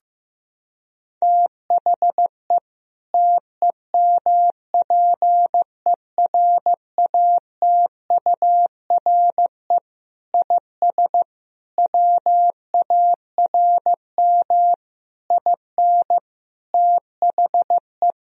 Morse code, heard as a keyed tone, about 15 wpm, 700 Hz. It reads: THE TEMPERATURE IS WARM IN THE
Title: Sentences from Top 1000 Words - Review Part 4 - 15wpm